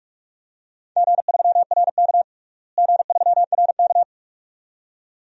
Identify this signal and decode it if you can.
Morse code, heard as a keyed tone, about 46 wpm, 700 Hz. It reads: G3RK G3RK
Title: DXCC-100 Call Signs - 4 Characters - Repeated 2x - 45wpm